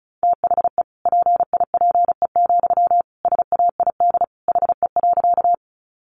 Morse code, heard as a keyed tone, about 34 wpm, 700 Hz. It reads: THE PIPE, SAID HE.
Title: Sentences from Top 1500 Words 35wpm